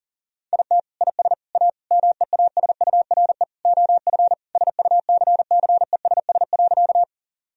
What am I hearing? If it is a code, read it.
Morse code, heard as a keyed tone, about 37 wpm, 700 Hz. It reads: IT IS A MEASURE OF SUCCESS.
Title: Sentences from Top 1000 Words - Review Part 2 - 40wpm